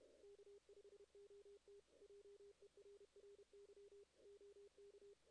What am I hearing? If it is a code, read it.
Morse code, heard as a keyed tone, about 30 wpm, 400 Hz. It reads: K5OT JERRY OK
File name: 2016 JAN CW NAQP / 2016 Jan naqp k5ot - 0-11